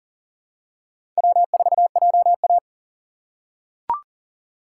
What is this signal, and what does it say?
Morse code, heard as a keyed tone, about 40 wpm, 700 Hz. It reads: W4JA
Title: USA Call Signs - 6x Character Spacing - 40wpm